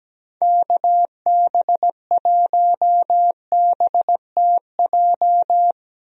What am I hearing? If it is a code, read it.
Morse code, heard as a keyed tone, about 17 wpm, 700 Hz. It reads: KB1BTJ
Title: USA Call Signs - 6 Characters - 17wpm